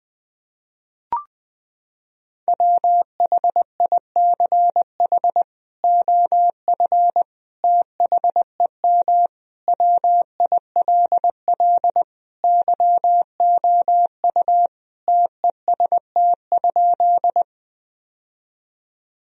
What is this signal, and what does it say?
Morse code, heard as a keyed tone, about 20 wpm, 700 Hz. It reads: WHICH OF THEM WILL YOU TEST?
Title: Sentences from Top 1000 Words - Review Part 3 - 20wpm